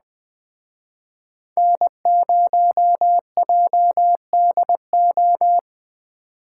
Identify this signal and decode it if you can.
Morse code, heard as a keyed tone, about 21 wpm, 700 Hz. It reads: N0JDO